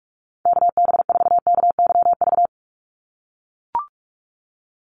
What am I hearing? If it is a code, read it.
Morse code, heard as a keyed tone, about 44 wpm, 700 Hz. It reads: KB4KYV
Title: USA Call Signs 45wpm